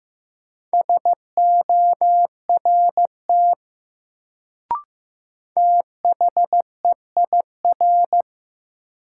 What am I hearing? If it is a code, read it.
Morse code, heard as a keyed tone, about 15 wpm, 700 Hz. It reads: SORT THEIR
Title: Sets of 2 Words - New Word Drill - 15wpm